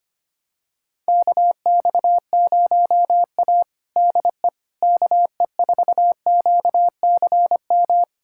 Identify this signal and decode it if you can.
Morse code, heard as a keyed tone, about 25 wpm, 700 Hz. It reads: KX0A DE KE4QCM
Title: USA Callsign DE Callsign 25wpm